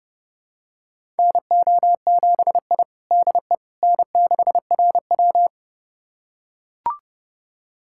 Morse code, heard as a keyed tone, about 30 wpm, 700 Hz. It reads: NO7I DE N6RW
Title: USA Callsign DE Callsign - 4 Characters - 30wpm